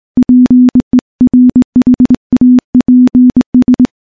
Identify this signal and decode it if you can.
Morse code, heard as a keyed tone, about 21 wpm, 250 Hz. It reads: PERHAPS
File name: morse / morse-training-20WPM